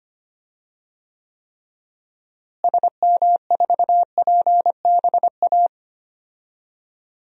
Morse code, heard as a keyed tone, about 25 wpm, 700 Hz. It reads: SM4PBA
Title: DXCC-100 Call Signs - 6 Characters - Repeated 3x - 25wpm